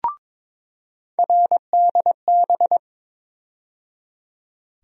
Morse code, heard as a keyed tone, about 22 wpm, 700 Hz. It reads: RDB
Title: USA Call Sign Postfixes - 3 Letters 22wpm